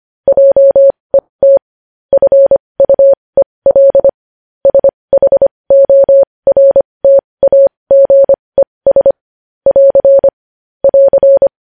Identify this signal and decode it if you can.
Morse code, heard as a keyed tone, about 25 wpm, spaced as for 20 wpm, 550 Hz. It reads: JET FUEL SHORTAGES <AR> <AR>
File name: podcast / News headlines-1x-25x20